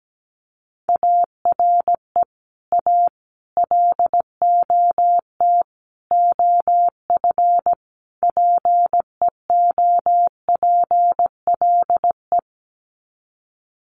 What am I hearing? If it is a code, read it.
Morse code, heard as a keyed tone, about 17 wpm, 700 Hz. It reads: ARE A LOT OF PEOPLE
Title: Sets of 5 Words 17wpm